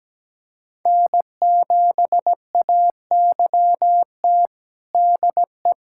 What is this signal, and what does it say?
Morse code, heard as a keyed tone, about 17 wpm, 700 Hz. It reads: N7AYT DE
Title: USA Callsign DE Callsign - 5 Characters - 17wpm